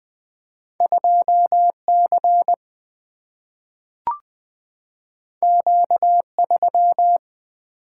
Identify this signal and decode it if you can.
Morse code, heard as a keyed tone, about 20 wpm, 700 Hz. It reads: 2 C Q 3
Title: Single-Letter Single-Number 20wpm